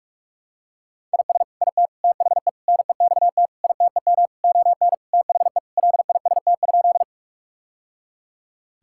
Morse code, heard as a keyed tone, about 46 wpm, 700 Hz. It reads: IS IT THE NEXT ITEM ON THE LIST?
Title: Sentences from Top 1000 Words - Review Part 6 - 45wpm